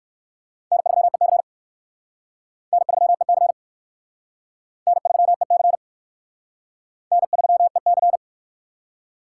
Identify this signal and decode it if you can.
Morse code, heard as a keyed tone, about 54 wpm, 700 Hz. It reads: N3EC N3EC N3EC N3EC
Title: Speed Racing - USA 1x2 Call Signs 45wpm